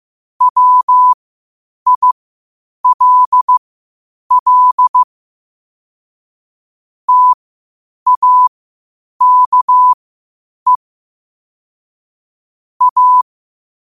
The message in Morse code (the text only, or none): WILL TAKE A